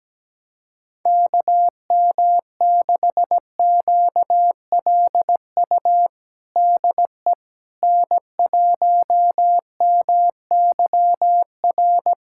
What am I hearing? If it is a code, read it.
Morse code, heard as a keyed tone, about 17 wpm, 700 Hz. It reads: KM6QLU DE N1MYR
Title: USA Callsign DE Callsign 17wpm